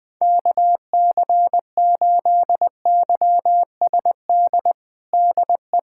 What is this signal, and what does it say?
Morse code, heard as a keyed tone, about 21 wpm, 700 Hz. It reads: KC8YSD DE
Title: USA Callsign DE Callsign - 6 Characters - 20wpm